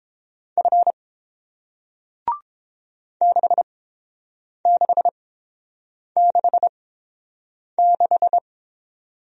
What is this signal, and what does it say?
Morse code, heard as a keyed tone, about 29 wpm, spaced as for 7 wpm, 700 Hz. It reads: F 6666